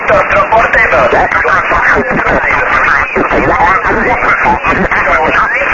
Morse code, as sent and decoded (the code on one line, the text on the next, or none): none